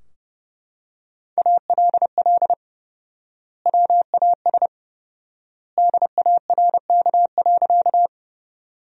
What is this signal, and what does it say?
Morse code, heard as a keyed tone, about 32 wpm, 700 Hz. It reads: ALL WAS DARK.
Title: 4x Word-Spacing - Sentences from Top 500 Words - 30wpm